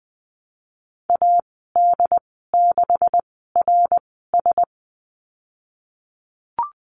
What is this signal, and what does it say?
Morse code, heard as a keyed tone, about 20 wpm, spaced as for 15 wpm, 700 Hz. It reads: AD6RS
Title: 147 Lesson 24g - Letter y - Call Signs 20wpm